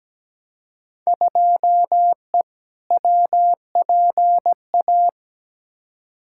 Morse code, heard as a keyed tone, about 17 wpm, 700 Hz. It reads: 2E WPA
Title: Field Day Exchanges - 17wpm